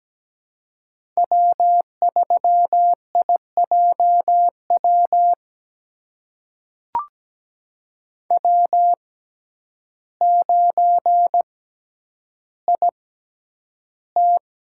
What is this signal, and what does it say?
Morse code, heard as a keyed tone, about 17 wpm, spaced as for 9 wpm, 700 Hz. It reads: W3IJW W9IT
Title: USA Call Signs - 5 Characters - 6x Character Spacing - 17wpm